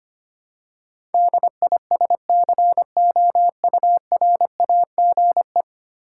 Morse code, heard as a keyed tone, about 25 wpm, 700 Hz. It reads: DISCOURAGE
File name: practice / 25wpm / Ten Letter Words - 25wpm